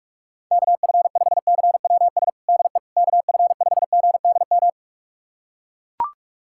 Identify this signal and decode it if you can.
Morse code, heard as a keyed tone, about 41 wpm, 700 Hz. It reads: KF5CWS DE KF5GDM
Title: USA Callsign DE Callsign - 6 Characters - 45wpm